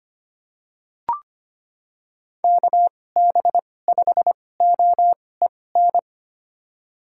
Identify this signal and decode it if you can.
Morse code, heard as a keyed tone, about 25 wpm, spaced as for 18 wpm, 700 Hz. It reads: KB5OEN
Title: USA Call Signs - 2x Character Spacing - 25wpm